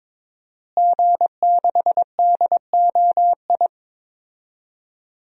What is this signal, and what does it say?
Morse code, heard as a keyed tone, about 22 wpm, 700 Hz. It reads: G6DOI
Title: DXCC-100 Call Signs - 5 Characters - Repeated 3x - 22wpm